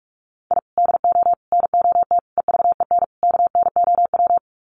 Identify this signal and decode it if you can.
Morse code, heard as a keyed tone, about 44 wpm, 700 Hz. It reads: I DO NOT EVEN KNOW
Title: Sets of 5 Words 45wpm